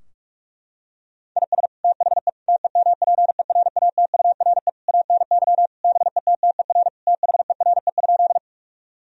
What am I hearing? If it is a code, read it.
Morse code, heard as a keyed tone, about 41 wpm, 700 Hz. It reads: IS THE TEMPERATURE ANY BETTER THERE?
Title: Sentences from Top 1000 Words - Review Part 4 - 45wpm